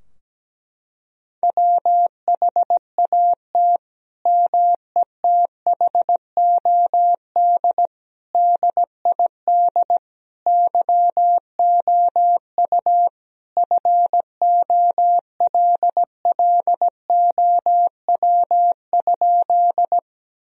Morse code, heard as a keyed tone, about 17 wpm, 700 Hz. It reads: WHAT METHOD DID YOU FOLLOW?
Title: Sentences from Top 1000 Words - Review Part 3 - 17wpm